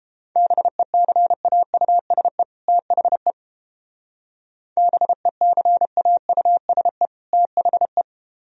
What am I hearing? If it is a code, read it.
Morse code, heard as a keyed tone, about 31 wpm, 700 Hz. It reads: BECAUSE THE BECAUSE THE